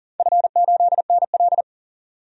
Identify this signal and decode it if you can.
Morse code, heard as a keyed tone, about 40 wpm, 700 Hz. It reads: F8NL
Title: DXCC-100 Call Signs - 4 Characters - 40wpm